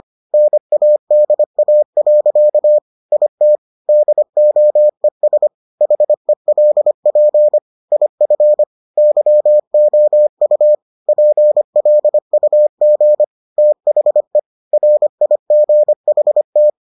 Morse code, heard as a keyed tone, about 25 wpm, 600 Hz. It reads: NADA. IT DOES HELP IF YOU PLUG THE RIGHT